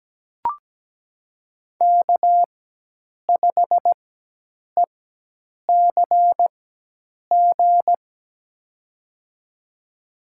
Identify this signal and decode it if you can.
Morse code, heard as a keyed tone, about 17 wpm, spaced as for 8 wpm, 700 Hz. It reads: K5ECG
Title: USA Call Signs - 5 Characters - 4x Character Spacing - 17wpm